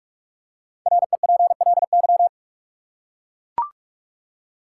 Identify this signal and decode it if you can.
Morse code, heard as a keyed tone, about 41 wpm, 700 Hz. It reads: REPLY
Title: Five Letter Words - 45wpm